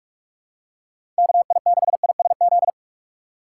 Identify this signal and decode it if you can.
Morse code, heard as a keyed tone, about 46 wpm, 700 Hz. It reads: KI6ISZ
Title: USA Call Signs - 6x Character Spacing - 45wpm